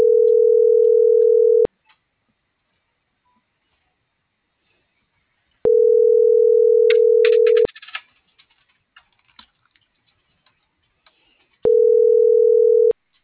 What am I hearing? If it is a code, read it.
Ringback tone (North America), 440 + 480 Hz.